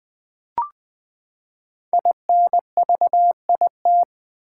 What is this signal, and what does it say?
Morse code, heard as a keyed tone, about 20 wpm, 700 Hz. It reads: INVIT
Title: Build A Word - 5 Letters At A Time - 10-Letter Words - 20wpm